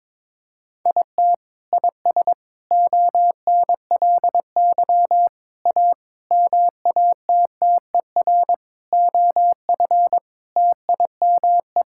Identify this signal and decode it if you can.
Morse code, heard as a keyed tone, about 22 wpm, 700 Hz. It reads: IT IS ONLY A MATTER OF TIME
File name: Sentences with 9-Word Phrases - 22wpm